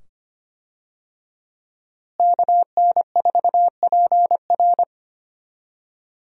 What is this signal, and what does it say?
Morse code, heard as a keyed tone, about 25 wpm, 700 Hz. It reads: KN4PR
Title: USA Call Signs - 5 Characters - Repeated 2x - 25wpm